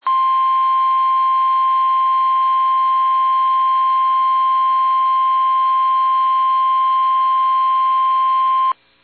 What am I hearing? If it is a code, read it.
Emergency Alert System (SAME) tones (the 1050 Hz attention tone). It is heard from 0.0 to 8.8 s.
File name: eas / KUSO / KUSO